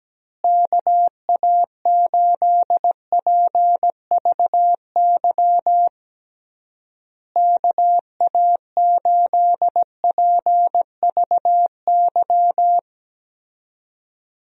Morse code, heard as a keyed tone, about 17 wpm, 700 Hz. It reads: KA8PVY KA8PVY
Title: USA Call Signs - 6 Characters - Repeated 3x - 17wpm